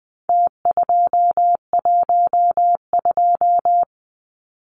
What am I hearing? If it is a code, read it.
Morse code, heard as a keyed tone, about 20 wpm, 700 Hz. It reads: T212
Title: Numbers - Contest Style - 4-Digit 20wpm